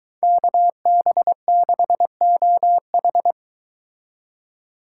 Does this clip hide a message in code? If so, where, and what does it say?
Yes: Morse code from 0.2 to 3.3 s, heard as a keyed tone, about 22 wpm, 700 Hz. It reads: KB6OH